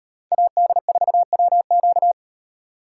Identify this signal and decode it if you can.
Morse code, heard as a keyed tone, about 38 wpm, 700 Hz. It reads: AD4WQ